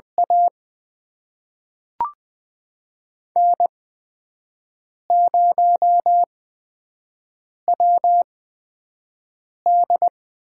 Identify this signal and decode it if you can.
Morse code, heard as a keyed tone, about 20 wpm, spaced as for 5 wpm, 700 Hz. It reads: A N0WD
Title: USA Call Signs - 4 Characters - 8x Character Spacing - 20wpm